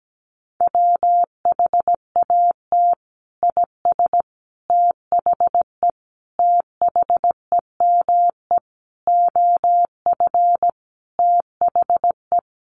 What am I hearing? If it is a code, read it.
Morse code, heard as a keyed tone, about 17 wpm, 700 Hz. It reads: WHAT IS THE THEME OF THE
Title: Sentences from Top 1100 Words 17wpm